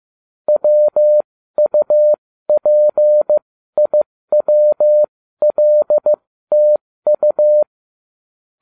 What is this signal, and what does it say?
Morse code, heard as a keyed tone, about 15 wpm, spaced as for 13 wpm, 600 Hz. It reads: WUPIWLTU
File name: cw / ubacw12 1